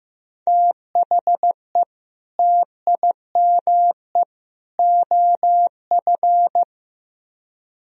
Morse code, heard as a keyed tone, about 17 wpm, 700 Hz. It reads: THE TIME OF
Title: Speed-Racing - Sets of 3 Words 15wpm